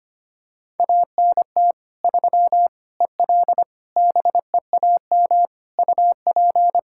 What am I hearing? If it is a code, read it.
Morse code, heard as a keyed tone, about 25 wpm, 700 Hz. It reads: ANT 3 EL BEAM UP
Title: Practice QSO - Exchange 2 - Info - 25wpm